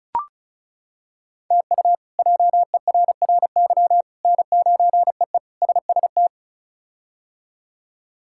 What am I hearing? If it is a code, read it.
Morse code, heard as a keyed tone, about 36 wpm, 700 Hz. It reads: TU JERRY N9EE SST
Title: K1USN SST QSO Practice - 35wpm